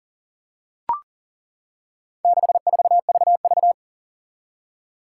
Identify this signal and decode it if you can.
Morse code, heard as a keyed tone, about 40 wpm, 700 Hz. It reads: 64VV
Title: Four Challenging Letter-Number Combinations 40wpm